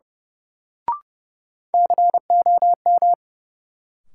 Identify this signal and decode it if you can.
Morse code, heard as a keyed tone, about 30 wpm, 700 Hz. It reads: COM
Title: Sentences from Top 1300 Words - New Word Drill - 30wpm